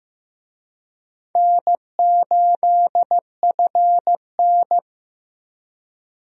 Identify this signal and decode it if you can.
Morse code, heard as a keyed tone, about 15 wpm, 700 Hz. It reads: N8FN
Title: USA Call Signs - 4 Characters - Repeated 2x - 15wpm